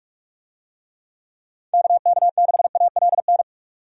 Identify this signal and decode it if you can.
Morse code, heard as a keyed tone, about 42 wpm, 700 Hz. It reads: KK6ALN
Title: USA Call Signs - 6 Characters - 45wpm